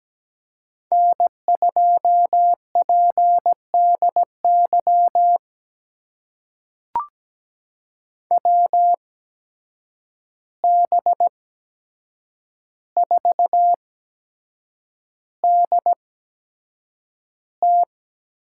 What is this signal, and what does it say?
Morse code, heard as a keyed tone, about 17 wpm, spaced as for 7 wpm, 700 Hz. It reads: N2PDY WB4DT
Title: USA Call Signs - 5 Characters - 8x Character Spacing - 17wpm